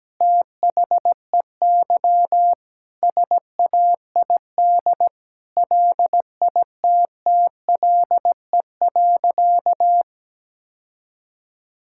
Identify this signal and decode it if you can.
Morse code, heard as a keyed tone, about 17 wpm, 700 Hz. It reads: THEY SAID LITTLE.
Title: Sentences from Top 1600 Words 17wpm